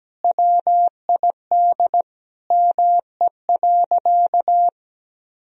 Morse code, heard as a keyed tone, about 17 wpm, 700 Hz. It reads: WID ME.
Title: Sentences from Top 200 Words and Abbreviations 17wpm